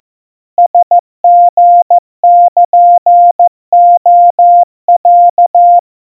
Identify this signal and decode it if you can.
Morse code (Japanese Wabun code), heard as a keyed tone, about 14 wpm, 700 Hz. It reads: ラリルレロ
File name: kawa / Morse sample / aiueo 50 cpm 700 Hz Crispy